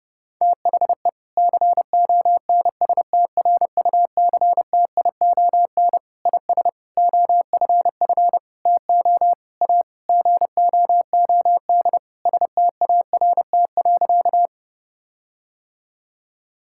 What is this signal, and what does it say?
Morse code, heard as a keyed tone, about 30 wpm, 700 Hz. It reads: THE CONSTRUCTION IS OFF TO A GOOD START.
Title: Sentences from Top 1600 Words 30wpm